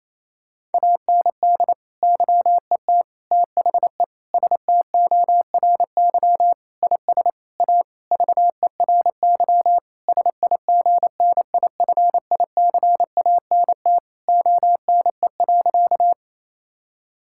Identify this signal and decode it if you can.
Morse code, heard as a keyed tone, about 28 wpm, 700 Hz. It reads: AND YET THE STORY IS A VERY SIGNIFICANT ONE.
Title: Sentences from Top 1000 Words - Review Part 6 - 28wpm